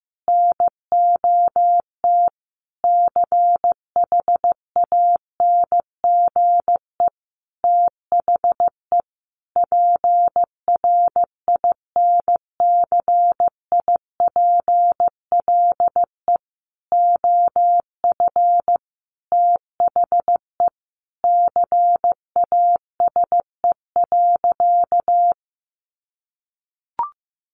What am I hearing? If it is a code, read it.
Morse code, heard as a keyed tone, about 15 wpm, 700 Hz. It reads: NOT CHANGE THE PRINCIPLE OF THE CASE.
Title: Sentences from Top 1400 Words 15wpm